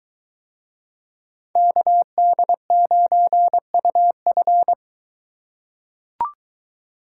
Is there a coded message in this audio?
Yes: Morse code from 1.6 to 4.7 s, heard as a keyed tone, about 23 wpm, 700 Hz. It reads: KD9UF